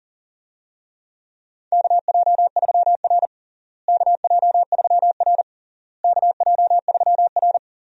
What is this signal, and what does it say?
Morse code, heard as a keyed tone, about 40 wpm, 700 Hz. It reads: KJ3R KJ3R KJ3R
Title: USA Call Signs - 4 Characters - Repeated 3x - 40wpm